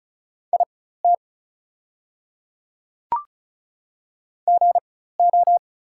Morse code, heard as a keyed tone, about 36 wpm, spaced as for 16 wpm, 700 Hz. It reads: IT GO
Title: 4x Character Spacing - Top 100 Words - 35wpm